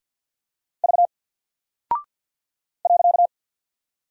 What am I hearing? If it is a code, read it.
Morse code, heard as a keyed tone, about 48 wpm, 700 Hz. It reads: V .
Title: Single Character (Letter:Number:Punctuation:Prosign) Alt 50wpm